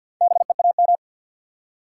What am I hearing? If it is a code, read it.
Morse code, heard as a keyed tone, about 69 wpm, spaced as for 59 wpm, 700 Hz. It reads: BEAM
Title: Speed-Racing Top 180 QSO Elements 50wpm